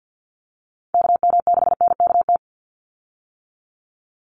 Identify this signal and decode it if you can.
Morse code, heard as a keyed tone, about 50 wpm, 700 Hz. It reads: KM6NKT
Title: USA Call Signs 50wpm